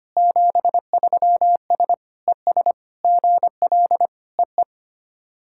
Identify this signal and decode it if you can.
Morse code, heard as a keyed tone, about 25 wpm, 690 Hz. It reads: 73S ES GL EE
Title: SOTA QSO Practice 25wpm